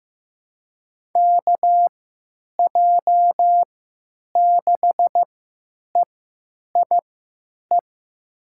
Morse code, heard as a keyed tone, about 15 wpm, spaced as for 9 wpm, 700 Hz. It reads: KJ6EIE
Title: USA Call Signs - 6 Characters - 3x Character Spacing - 15wpm